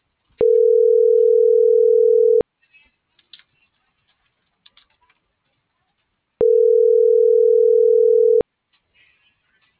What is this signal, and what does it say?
Ringback tone (North America), 440 + 480 Hz.
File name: CENTRAL MULTIMODAL / out josimar.cruz - .69911